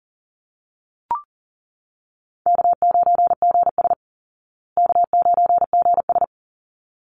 Morse code, heard as a keyed tone, about 40 wpm, 700 Hz. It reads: K9GS K9GS
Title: Call Signs Repeated 3x - 40wpm